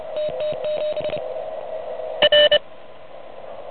Morse code, heard as a keyed tone, about 29 wpm, 590 Hz. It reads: TT7 R
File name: truSDX / MWC29.8